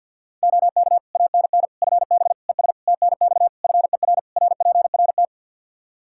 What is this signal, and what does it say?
Morse code, heard as a keyed tone, about 50 wpm, 700 Hz. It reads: OK ANN FB ES TNX FER RPRT
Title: K1USN SST QSO Practice 50wpm